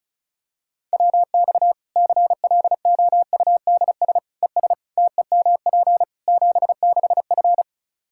Morse code, heard as a keyed tone, about 34 wpm, 700 Hz. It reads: WX CLOUDS ES TEMP 76F
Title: K1USN SST QSO Practice 35wpm